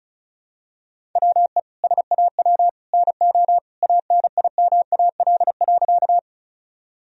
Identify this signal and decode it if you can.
Morse code, heard as a keyed tone, about 34 wpm, 700 Hz. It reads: WE SAW NO ANIMAL.
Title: Sentences from Top 1000 Words - Review Part 1 - 35wpm